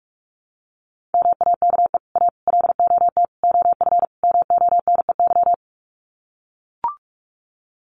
Morse code, heard as a keyed tone, about 44 wpm, 700 Hz. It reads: MAKE A LOT OF MONEY
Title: Sets of 5 Words Encore2 45wpm